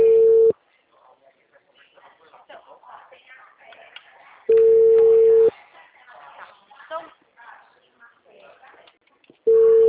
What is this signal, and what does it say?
Ringback tone (China, 450 Hz), with speech.
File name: out- -2006- - - .28232